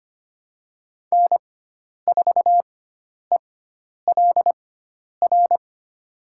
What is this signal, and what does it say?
Morse code, heard as a keyed tone, about 25 wpm, spaced as for 10 wpm, 700 Hz. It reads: N4ELR
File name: Lesson 16g - Letter c - Call Signs 25wpm